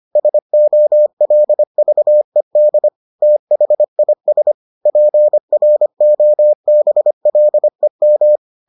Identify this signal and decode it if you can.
Morse code, heard as a keyed tone, about 25 wpm, 600 Hz. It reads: SOLVED THIS PROBLEM